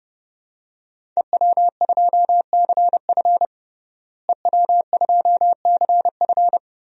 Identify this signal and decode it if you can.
Morse code, heard as a keyed tone, about 32 wpm, 700 Hz. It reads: EW2CF EW2CF
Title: DXCC-100 Call Signs - 5 Characters - Repeated 3x - 30wpm